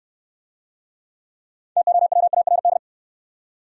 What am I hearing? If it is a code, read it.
Morse code, heard as a keyed tone, about 61 wpm, 700 Hz. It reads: TOWARD